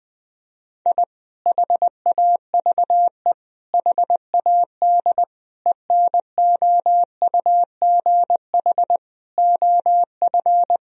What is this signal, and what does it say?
Morse code, heard as a keyed tone, about 20 wpm, 700 Hz. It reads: I HAVE HAD ENOUGH OF
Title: Sentences from Top 1000 Words - Review Part 3 - 20wpm